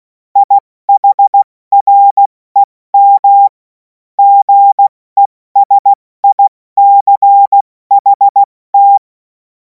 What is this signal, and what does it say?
Morse code, heard as a keyed tone, about 16 wpm, 800 Hz. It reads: IHREM GESICHT
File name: Morsekurs / Witze / Witz 11. August 2026 70 F80